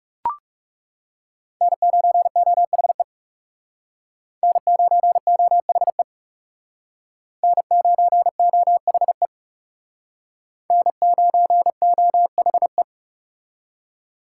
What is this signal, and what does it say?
Morse code, heard as a keyed tone, about 36 wpm, 700 Hz. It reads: N9OHE N9OHE N9OHE N9OHE
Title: Speed Racing - USA 1x3 Call Signs 30wpm